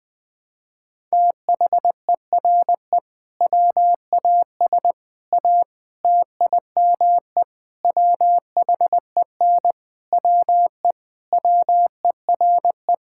Morse code, heard as a keyed tone, about 20 wpm, 700 Hz. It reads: THERE WAS A TIME WHEN WE WERE
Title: Sentences with 5-Word Phrases - 20wpm